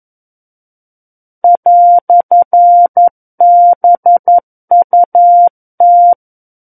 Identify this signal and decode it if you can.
Morse code, heard as a keyed tone, about 11 wpm, 700 Hz. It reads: "BUT
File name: 11 WPM